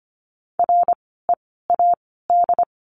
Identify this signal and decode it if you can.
Morse code, heard as a keyed tone, about 25 wpm, 700 Hz. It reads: R E A D
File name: course / 25:10wpm / 094 Lesson 16f - Letter c - Words 25wpm